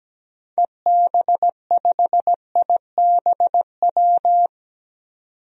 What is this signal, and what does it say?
Morse code, heard as a keyed tone, about 17 wpm, 700 Hz. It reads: EB5IBW
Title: DXCC-100 Call Signs - 6 Characters - 17wpm